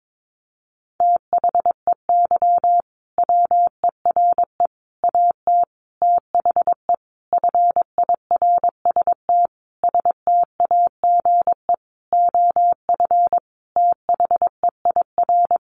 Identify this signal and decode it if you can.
Morse code, heard as a keyed tone, about 22 wpm, 700 Hz. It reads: THEY WERE AT THE FIRST STAGE OF THEIR
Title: Sentences from Top 1200 Words 22wpm